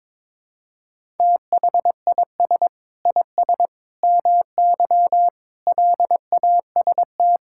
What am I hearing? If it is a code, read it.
Morse code, heard as a keyed tone, about 22 wpm, 700 Hz. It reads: THIS IS MY LAST
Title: Sentences from Top 1000 Words - Review Part 5 - 22wpm